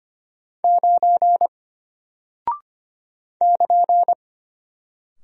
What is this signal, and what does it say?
Morse code, heard as a keyed tone, about 25 wpm, 700 Hz. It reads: 9 <KN>
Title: Single Character (Letter:Number:Punctuation:Prosign) Alt 25wpm